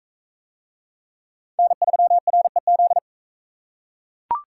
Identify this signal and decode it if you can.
Morse code, heard as a keyed tone, about 43 wpm, 700 Hz. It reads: N3REZ